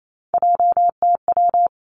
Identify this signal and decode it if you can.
Morse code, heard as a keyed tone, about 28 wpm, 700 Hz. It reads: JTW
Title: Three Challenging Letter-Number Combinations 28wpm